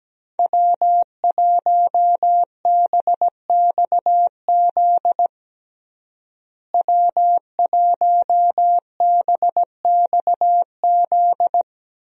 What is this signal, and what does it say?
Morse code, heard as a keyed tone, about 17 wpm, 700 Hz. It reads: W1BXZ W1BXZ
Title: USA Call Signs - 5 Characters - Repeated 2x - 17wpm